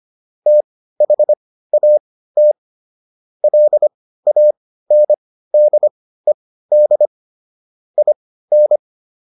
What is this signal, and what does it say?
Morse code, heard as a keyed tone, about 25 wpm, spaced as for 15 wpm, 600 Hz. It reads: THAT LANDED IN